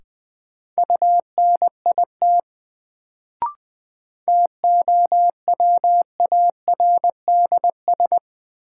Morse code, heard as a keyed tone, about 20 wpm, 700 Hz. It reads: UNIT TOWARDS
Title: Sentences from Top 1000 Words - Review Part 2 - New Word Drill - 20wpm